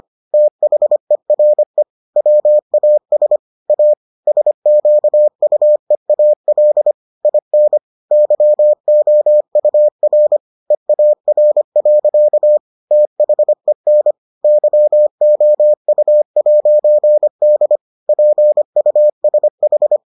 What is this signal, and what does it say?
Morse code, heard as a keyed tone, about 25 wpm, 600 Hz. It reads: THERE WAS A SQUEAL IN YOUR EAR. THEN YOU'D PUSH